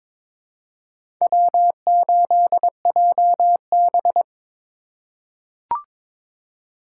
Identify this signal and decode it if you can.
Morse code, heard as a keyed tone, about 22 wpm, 700 Hz. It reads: W8JB
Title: USA Call Signs - 2x Character Spacing - 22wpm